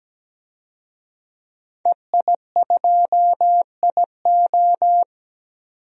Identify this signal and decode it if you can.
Morse code, heard as a keyed tone, about 17 wpm, 700 Hz. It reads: EI2IO
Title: DXCC-100 Call Signs - 5 Characters - 17wpm